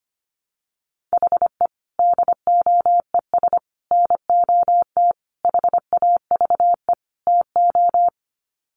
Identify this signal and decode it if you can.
Morse code, heard as a keyed tone, about 25 wpm, 700 Hz. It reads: HE DOES NOT HAVE TO
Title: Sets of 5 Words Encore 25wpm